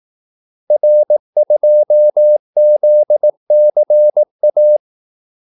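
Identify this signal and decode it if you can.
Morse code, heard as a keyed tone, about 18 wpm, 600 Hz. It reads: R2ZCA
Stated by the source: Morse call-sign practice at 18 wpm.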